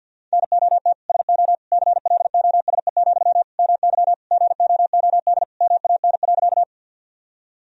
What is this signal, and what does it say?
Morse code, heard as a keyed tone, about 50 wpm, 700 Hz. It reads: NOT SO CLOSE, MY GOOD MAN.
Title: Sentences from Top 1000 Words - Review Part 1 - 50wpm